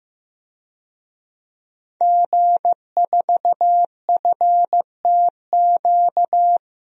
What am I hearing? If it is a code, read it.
Morse code, heard as a keyed tone, about 16 wpm, 700 Hz. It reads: G4FTQ
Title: DXCC-100 Call Signs - 5 Characters - Repeated 2x - 15wpm